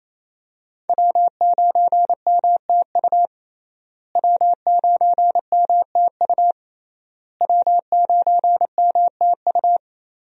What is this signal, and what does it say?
Morse code, heard as a keyed tone, about 28 wpm, 700 Hz. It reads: W9MTU W9MTU W9MTU
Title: USA Call Signs - 5 Characters - Repeated 3x - 28wpm